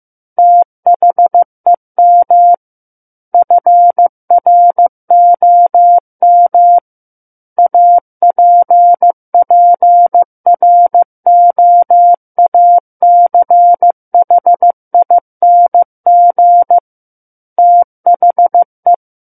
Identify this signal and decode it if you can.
Morse code, heard as a keyed tone, about 15 wpm, 700 Hz. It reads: THEM FROM APPROACHING THE
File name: Chapter 6 - 15 WPM